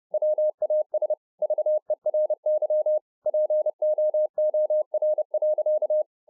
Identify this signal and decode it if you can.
Morse code, heard as a keyed tone, about 30 wpm, 610 Hz. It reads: WAS VERY POOR.